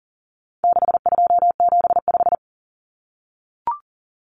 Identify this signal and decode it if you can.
Morse code, heard as a keyed tone, about 40 wpm, 700 Hz. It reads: 6275
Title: Common Cut Numbers - 4-Digit 40wpm